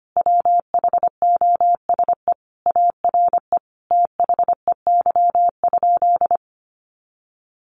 Morse code, heard as a keyed tone, about 25 wpm, 700 Hz. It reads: WHOSE ARE THEY?
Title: Sentences from Top 1200 Words 25wpm